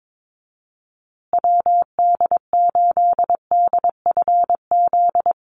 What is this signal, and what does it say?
Morse code, heard as a keyed tone, about 22 wpm, 700 Hz. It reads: WD8DFZ
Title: USA Call Signs Repeated 2x 22wpm